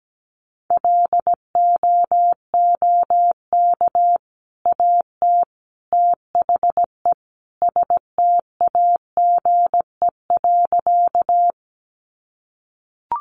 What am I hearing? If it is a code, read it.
Morse code, heard as a keyed tone, about 17 wpm, 700 Hz. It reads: LOOK AT THE STAGE.
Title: Sentences from Top 1200 Words 17wpm